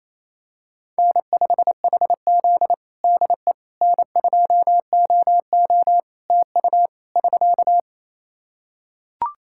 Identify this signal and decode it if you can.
Morse code, heard as a keyed tone, about 28 wpm, 700 Hz. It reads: N5HZ DE N2OO TU <SK>
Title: Practice QSO - Exchange 4 - Ending - 28wpm